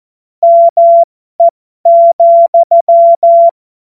Morse code, heard as a keyed tone, about 14 wpm, spaced as for 12 wpm, 680 Hz. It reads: ME,
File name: Morsekurs / Witze